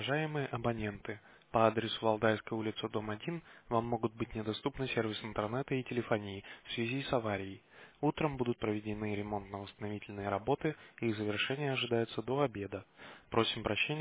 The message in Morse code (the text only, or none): none